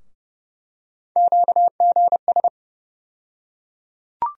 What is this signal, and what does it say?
Morse code, heard as a keyed tone, about 29 wpm, 700 Hz. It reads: QGS
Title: USA Call Sign Postfixes - 3 Letters 30wpm